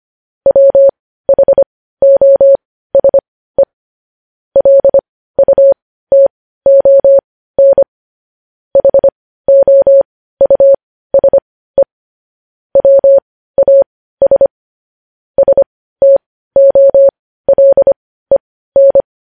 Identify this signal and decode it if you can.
Morse code, heard as a keyed tone, about 25 wpm, spaced as for 15 wpm, 550 Hz. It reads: WHOSE LUTON HOUSE WAS STOLEN
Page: news headlines in morse code at 15 wpm